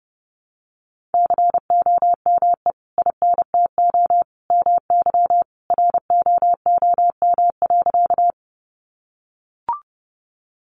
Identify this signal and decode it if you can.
Morse code, heard as a keyed tone, about 30 wpm, 700 Hz. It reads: COME INTO MY ROOM.
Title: Sentences from Top 600 Words 30wpm